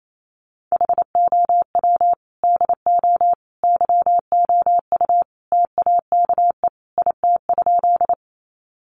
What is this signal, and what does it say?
Morse code, heard as a keyed tone, about 28 wpm, 700 Hz. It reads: HOW DO YOU TAKE IT?
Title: Sentences from Top 200 Words 28wpm